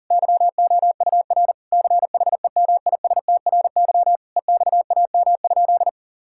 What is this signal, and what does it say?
Morse code, heard as a keyed tone, about 37 wpm, 700 Hz. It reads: YOUR CHEMISTRY EXAM?
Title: Sentences from Top 1000 Words - Review Part 6 - 40wpm